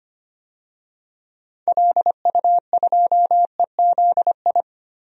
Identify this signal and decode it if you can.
Morse code, heard as a keyed tone, about 25 wpm, 700 Hz. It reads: LU2EZI
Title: DXCC-100 Call Signs - 6 Characters - Repeated 2x - 25wpm